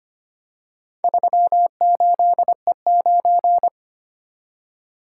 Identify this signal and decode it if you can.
Morse code, heard as a keyed tone, about 25 wpm, 700 Hz. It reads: 38E9
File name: practice / 25wpm / Four Letter-Number Combinations - 25wpm